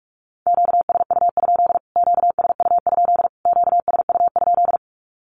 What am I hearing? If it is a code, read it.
Morse code, heard as a keyed tone, about 44 wpm, 700 Hz. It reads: QSU? QSU? QSU?
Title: Q-Codes Repeated 3x 45wpm